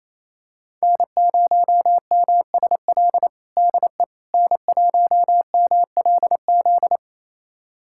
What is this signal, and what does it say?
Morse code, heard as a keyed tone, about 28 wpm, 700 Hz. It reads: N0MSL DE N1MLZ
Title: USA Callsign DE Callsign - 5 Characters - 28wpm